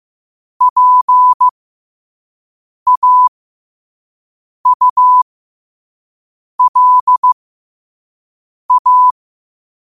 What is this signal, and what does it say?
Morse code, heard as a keyed tone, about 15 wpm, spaced as for 5 wpm, 1000 Hz. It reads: PAULA
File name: qotdfiles / 5wpm